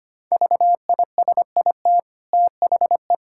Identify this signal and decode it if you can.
Morse code, heard as a keyed tone, about 25 wpm, 700 Hz. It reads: VISIT THE
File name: Easy Sentences with Hints - 25wpm